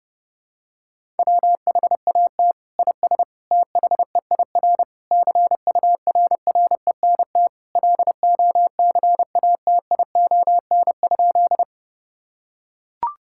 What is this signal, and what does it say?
Morse code, heard as a keyed tone, about 30 wpm, 700 Hz. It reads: WHAT IS THEIR CURRENT LOCATION?
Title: Sentences from Top 1000 Words - Review Part 6 - 30wpm